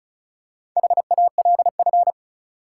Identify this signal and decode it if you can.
Morse code, heard as a keyed tone, about 36 wpm, 700 Hz. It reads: HALF
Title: Four Letter Words - 35wpm